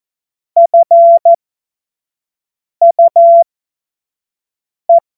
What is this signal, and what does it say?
Morse code, heard as a keyed tone, about 14 wpm, spaced as for 5 wpm, 680 Hz. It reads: FUE
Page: cw-witze